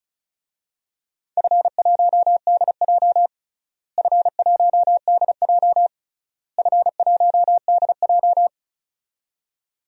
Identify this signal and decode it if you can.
Morse code, heard as a keyed tone, about 34 wpm, 700 Hz. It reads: F1DJ F1DJ F1DJ
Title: DXCC-100 Call Signs - 4 Characters - Repeated 3x - 35wpm